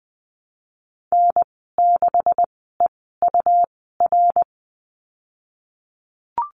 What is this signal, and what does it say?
Morse code, heard as a keyed tone, about 20 wpm, spaced as for 15 wpm, 700 Hz. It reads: N6EUR
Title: 186 Lesson 30g - Letter b - Call Signs 20wpm